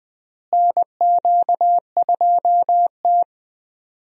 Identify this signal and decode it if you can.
Morse code, heard as a keyed tone, about 20 wpm, 700 Hz. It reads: NQ2T
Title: USA Call Signs - 4 Characters - Repeated 3x - 20wpm